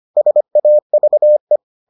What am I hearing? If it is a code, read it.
Morse code, heard as a keyed tone, about 25 wpm, 600 Hz. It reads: SAVE